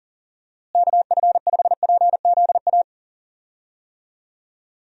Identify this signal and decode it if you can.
Morse code, heard as a keyed tone, about 40 wpm, 700 Hz. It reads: KF5PZA
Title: USA Call Signs - 6 Characters - Repeated 3x - 40wpm